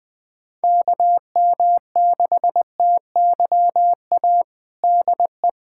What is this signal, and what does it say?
Morse code, heard as a keyed tone, about 20 wpm, 700 Hz. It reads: KM6TYA DE
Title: USA Callsign DE Callsign 20wpm